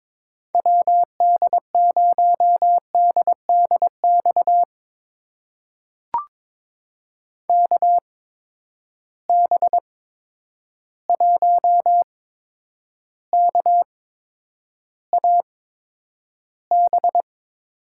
Morse code, heard as a keyed tone, about 22 wpm, spaced as for 9 wpm, 700 Hz. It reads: WD0DDX KB1KAB
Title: USA Call Signs - 6 Characters - 8x Character Spacing - 22wpm